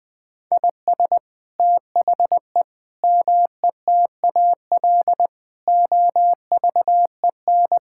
Morse code, heard as a keyed tone, about 20 wpm, 700 Hz. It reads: IS THE METAL OVEN
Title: Sentences from Top 1000 Words - Review Part 6 - 20wpm